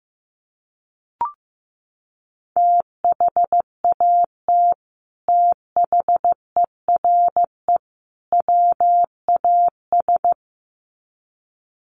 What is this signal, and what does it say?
Morse code, heard as a keyed tone, about 15 wpm, 700 Hz. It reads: THAT THERE WAS
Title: Sets of 3 Words 15wpm